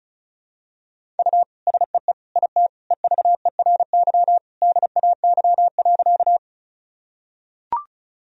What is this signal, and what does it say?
Morse code, heard as a keyed tone, about 35 wpm, 700 Hz. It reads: U SEE IT EVERY DAY.
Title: Sentences from Top 200 Words and Abbreviations 35wpm